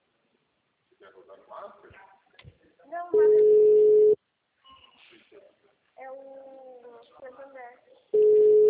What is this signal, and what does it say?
Ringback tone (Europe, 425 Hz), with speech.
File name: out- -2018- - - .63683